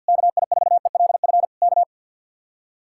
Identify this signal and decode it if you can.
Morse code, heard as a keyed tone, about 62 wpm, 700 Hz. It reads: KI4ELF K
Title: Speed-Racing Common QSO Phrases - 50wpm